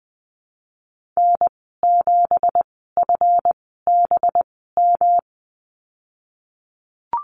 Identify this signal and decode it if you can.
Morse code, heard as a keyed tone, about 20 wpm, spaced as for 15 wpm, 700 Hz. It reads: N7FBM
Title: 186 Lesson 30g - Letter b - Call Signs 20wpm